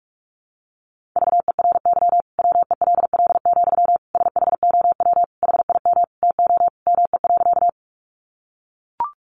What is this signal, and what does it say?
Morse code, heard as a keyed tone, about 44 wpm, 700 Hz. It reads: VERY WELL, SHOW HIM TO ME.
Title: Sentences from Top 200 Words 45wpm